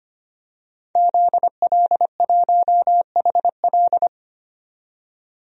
Morse code, heard as a keyed tone, about 25 wpm, 700 Hz. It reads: ZL1HL
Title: DXCC-100 Call Signs - 5 Characters - 25wpm